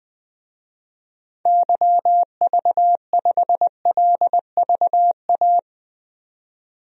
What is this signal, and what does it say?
Morse code, heard as a keyed tone, about 20 wpm, 700 Hz. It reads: YV5LVA
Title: DXCC-100 Call Signs - 6 Characters - 20wpm